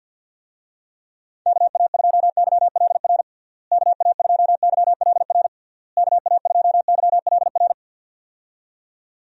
Morse code, heard as a keyed tone, about 47 wpm, 700 Hz. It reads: KA2YLR KA2YLR KA2YLR
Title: USA Call Signs - 6 Characters - Repeated 3x - 50wpm